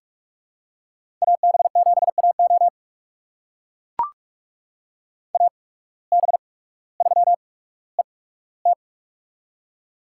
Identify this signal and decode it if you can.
Morse code, heard as a keyed tone, about 44 wpm, spaced as for 19 wpm, 700 Hz. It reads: AB7AO AB3ET
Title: USA Call Signs - 5 Characters - 8x Character Spacing - 45wpm